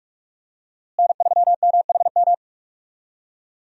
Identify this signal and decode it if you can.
Morse code, heard as a keyed tone, about 46 wpm, 700 Hz. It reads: N3MHM
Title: USA Call Signs - 5 Characters - Repeated 3x - 45wpm